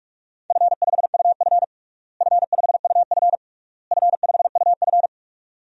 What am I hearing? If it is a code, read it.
Morse code, heard as a keyed tone, about 43 wpm, 700 Hz. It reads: F5UF F5UF F5UF
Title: DXCC-100 Call Signs - 4 Characters - Repeated 3x - 45wpm